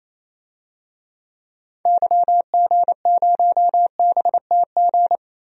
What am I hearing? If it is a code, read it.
Morse code, heard as a keyed tone, about 28 wpm, 700 Hz. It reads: YG0BTG
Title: DXCC-100 Call Signs - 6 Characters - 28wpm